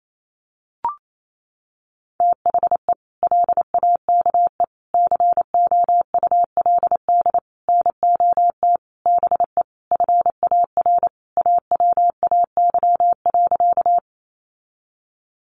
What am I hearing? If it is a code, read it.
Morse code, heard as a keyed tone, about 28 wpm, 700 Hz. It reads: THE LAKE COULD NOT BE FAR AWAY.
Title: Sentences from Top 1100 Words 28wpm